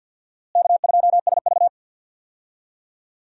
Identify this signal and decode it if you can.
Morse code, heard as a keyed tone, about 50 wpm, 700 Hz. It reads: K2SV
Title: USA Call Signs - 4 Characters - Repeated 3x - 50wpm